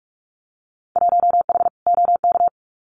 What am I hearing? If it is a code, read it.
Morse code, heard as a keyed tone, about 46 wpm, 700 Hz. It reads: 1H OK
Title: Winter Field Day Exchanges 45wpm